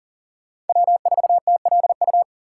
Morse code, heard as a keyed tone, about 52 wpm, 700 Hz. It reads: W4TLU